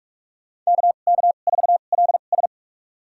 Morse code, heard as a keyed tone, about 45 wpm, spaced as for 32 wpm, 700 Hz. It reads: KK4LS
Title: USA Call Signs - 5 Characters - 2x Character Spacing - 45wpm